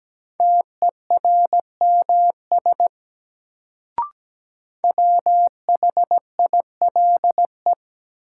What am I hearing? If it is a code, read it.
Morse code, heard as a keyed tone, about 17 wpm, 700 Hz. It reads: TERMS WHILE
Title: Sets of 3 Words - New Word Drill - 17wpm